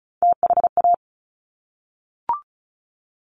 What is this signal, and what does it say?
Morse code, heard as a keyed tone, about 36 wpm, 690 Hz. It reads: THA
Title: Top Three Letter Combinations in Words 35wpm